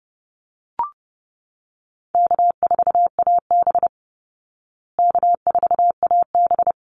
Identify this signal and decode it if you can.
Morse code, heard as a keyed tone, about 30 wpm, 700 Hz. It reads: K4AB K4AB
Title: Easy Call Signs Repeated 3x - 30wpm